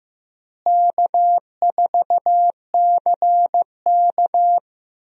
Morse code, heard as a keyed tone, about 15 wpm, 700 Hz. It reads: K4CK
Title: USA Call Signs - 4 Characters - Repeated 2x - 15wpm